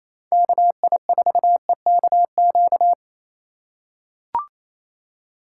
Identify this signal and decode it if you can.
Morse code, heard as a keyed tone, about 28 wpm, 700 Hz. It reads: KI4EKQ
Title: USA Call Signs - 6 Characters - 2x Character Spacing - 28wpm